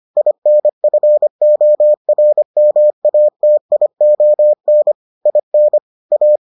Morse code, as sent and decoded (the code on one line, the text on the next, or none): .. -. ..-. --- .-. -- .- - .. --- -. / .. -. / .-
INFORMATION IN A